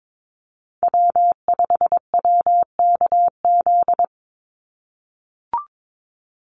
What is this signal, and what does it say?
Morse code, heard as a keyed tone, about 22 wpm, 700 Hz. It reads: W5WKZ
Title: USA Call Signs 22wpm